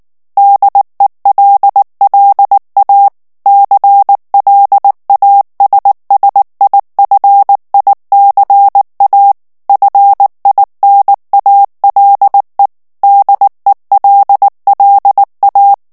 Morse code, heard as a keyed tone, about 19 wpm, 800 Hz. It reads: DELLA CLASSIFICA FINALE DELLA
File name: morse mp3 / ITALIEN texte clair 2 vitesse 18 mots minute